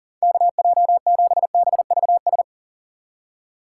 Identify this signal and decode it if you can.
Morse code, heard as a keyed tone, about 40 wpm, 700 Hz. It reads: KJ7BVS
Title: USA Call Signs - 6 Characters - 3x Character Spacing - 40wpm